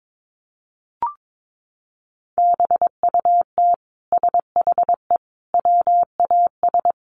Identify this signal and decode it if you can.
Morse code, heard as a keyed tone, about 22 wpm, 700 Hz. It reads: BUT SHE WAS
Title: Sentences from Top 1200 Words 22wpm